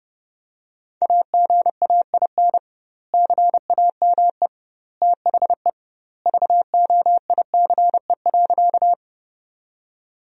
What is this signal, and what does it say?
Morse code, heard as a keyed tone, about 30 wpm, 700 Hz. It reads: AGAIN CAME THE VOICE.
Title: 2x Word-Spacing - Sentences from Top 500 Words - 30wpm